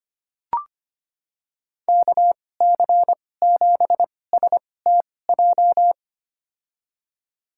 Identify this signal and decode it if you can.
Morse code, heard as a keyed tone, about 25 wpm, spaced as for 18 wpm, 700 Hz. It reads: KC7STJ
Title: USA Call Signs - 6 Characters - 2x Character Spacing - 25wpm